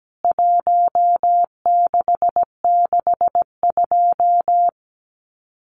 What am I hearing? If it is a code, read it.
Morse code, heard as a keyed tone, about 17 wpm, 700 Hz. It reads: 1662
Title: CWT Exchange 17wpm